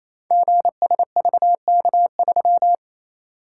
Morse code, heard as a keyed tone, about 28 wpm, 690 Hz. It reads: GSVK3
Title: Code Groups - 5 Letter:Numbers - 28wpm